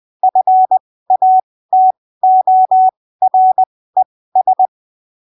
Morse code, heard as a keyed tone, about 20 wpm, spaced as for 15 wpm, 750 Hz. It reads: FATORES
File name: arquivos / texto / texto 22 20ppm